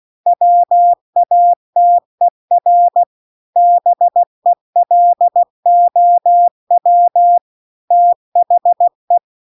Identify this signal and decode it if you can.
Morse code, heard as a keyed tone, about 16 wpm, 700 Hz. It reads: WATER BELOW THE
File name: CW-The-Big-Twoheaded-River-2-16WPM